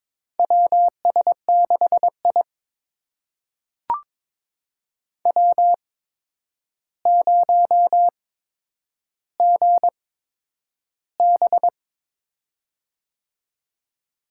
Morse code, heard as a keyed tone, about 22 wpm, spaced as for 9 wpm, 700 Hz. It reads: WS6I W0GB
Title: USA Call Signs - 4 Characters - 8x Character Spacing - 22wpm